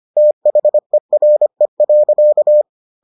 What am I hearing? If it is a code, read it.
Morse code, heard as a keyed tone, about 25 wpm, 600 Hz. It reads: THERE.